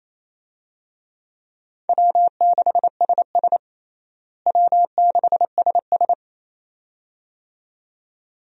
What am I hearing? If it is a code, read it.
Morse code, heard as a keyed tone, about 28 wpm, 700 Hz. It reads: W6SS W6SS
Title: USA Call Signs - 4 Characters - Repeated 2x - 28wpm